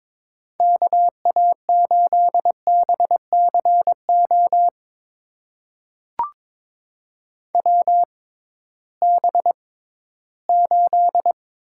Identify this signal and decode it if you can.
Morse code, heard as a keyed tone, about 22 wpm, spaced as for 14 wpm, 700 Hz. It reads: KA8BCO WB8
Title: USA Call Signs - 6 Characters - 6x Character Spacing - 22wpm